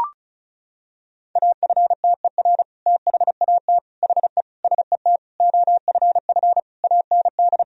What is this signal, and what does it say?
Morse code, heard as a keyed tone, about 34 wpm, 700 Hz. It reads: AFTER THAT HE SET OFF AND
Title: Sentences from Top 1000 Words - Review Part 1 - 35wpm